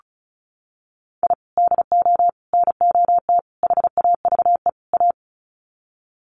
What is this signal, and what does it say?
Morse code, heard as a keyed tone, about 34 wpm, 700 Hz. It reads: I DO NOT HAVE A
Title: Sets of 5 Words 35wpm